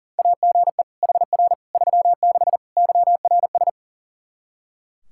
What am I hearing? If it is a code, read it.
Morse code, heard as a keyed tone, about 40 wpm, 700 Hz. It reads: AGE HR 36 YRS
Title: Practice QSO - Exchange 3 - More Info - 40wpm